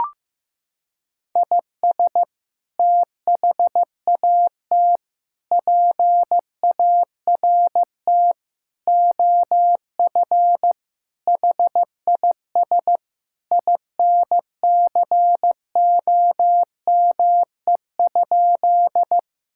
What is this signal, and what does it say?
Morse code, heard as a keyed tone, about 16 wpm, 700 Hz. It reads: IS THAT PART OF HIS INCOME?
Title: Sentences from Top 1000 Words - Review Part 5 - 15wpm